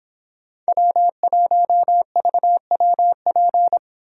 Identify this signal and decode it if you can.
Morse code, heard as a keyed tone, about 29 wpm, 700 Hz. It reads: W1VWP